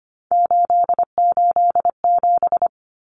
Morse code, heard as a keyed tone, about 25 wpm, 700 Hz. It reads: 887
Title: Common Cut Numbers - 3-Digit 25wpm